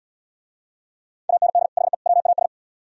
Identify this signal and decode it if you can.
Morse code, heard as a keyed tone, about 70 wpm, 700 Hz. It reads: AND HE WAS